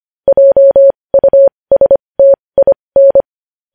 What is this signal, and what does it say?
Morse code, heard as a keyed tone, about 25 wpm, spaced as for 20 wpm, 550 Hz. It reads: JUSTIN
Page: news headlines in morse code at 20 wpm